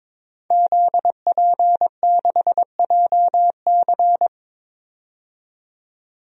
Morse code, heard as a keyed tone, about 22 wpm, 700 Hz. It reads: ZP6JC
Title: DXCC-100 Call Signs - 5 Characters - 22wpm